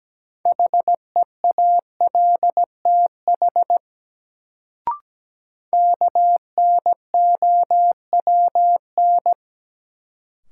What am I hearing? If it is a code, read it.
Morse code, heard as a keyed tone, about 17 wpm, 700 Hz. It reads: HEALTH KNOWN
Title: Sentences from Top 600 Words - New Word Drill - 17wpm